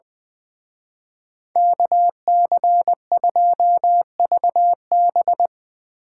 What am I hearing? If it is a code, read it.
Morse code, heard as a keyed tone, about 22 wpm, 700 Hz. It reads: KC2VB
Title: Speed Racing - USA 2x2 Call Signs 20wpm